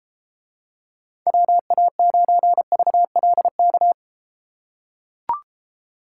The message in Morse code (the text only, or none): WA9VLK